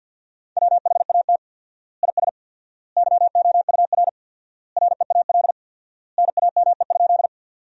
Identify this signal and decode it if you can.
Morse code, heard as a keyed tone, about 50 wpm, 700 Hz. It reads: WHAT IS YOUR REAL NAME?
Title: 4x Word-Spacing - Sentences from Top 300 Words - 50wpm